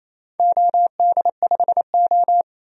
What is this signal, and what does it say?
Morse code, heard as a keyed tone, about 28 wpm, 700 Hz. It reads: OD5O
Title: DXCC-100 Call Signs - 4 Characters - 28wpm